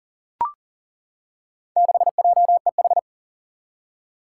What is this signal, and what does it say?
Morse code, heard as a keyed tone, about 40 wpm, 700 Hz. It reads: 6JEH
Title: Four Challenging Letter-Number Combinations 40wpm